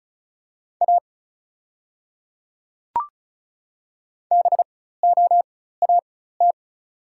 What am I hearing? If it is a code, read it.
Morse code, heard as a keyed tone, about 34 wpm, spaced as for 16 wpm, 690 Hz. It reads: A BOAT
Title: 4x Character Spacing - Top 500 Words - 35wpm